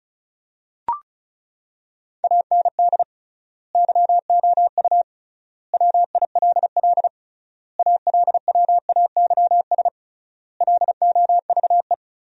Morse code, heard as a keyed tone, about 36 wpm, 700 Hz. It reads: AND YOU WILL ALWAYS LOVE
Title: 3x Word-Spacing - Sentences from Top 400 Words - 35wpm